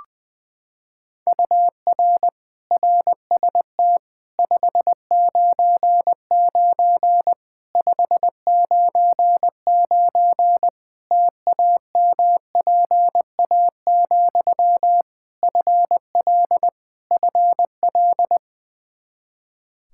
Morse code, heard as a keyed tone, about 20 wpm, 700 Hz. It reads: UR RST 599 599 TAMPA, FL FL
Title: K1USN SST QSO Practice 20wpm